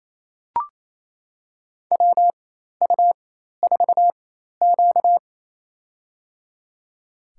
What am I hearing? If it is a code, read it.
Morse code, heard as a keyed tone, about 28 wpm, spaced as for 13 wpm, 700 Hz. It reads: WU4Q
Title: USA Call Signs - 4 Characters - 4x Character Spacing - 28wpm